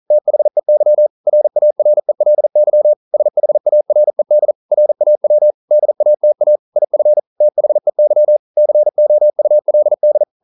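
Morse code, heard as a keyed tone, about 41 wpm, 600 Hz. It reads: THEY RARELY SHARED RAW DATA IF THEY COULD
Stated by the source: Morse code practice at 41 wpm, from CW med Gnister.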